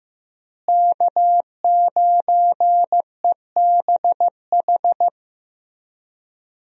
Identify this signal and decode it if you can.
Morse code, heard as a keyed tone, about 15 wpm, 700 Hz. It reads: K9EBH
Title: USA Call Signs - 5 Characters - Repeated 2x - 15wpm